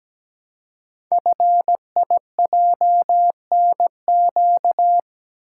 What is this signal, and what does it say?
Morse code, heard as a keyed tone, about 17 wpm, 700 Hz. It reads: FIJNQ
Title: Code Groups - 5 Letters - 17wpm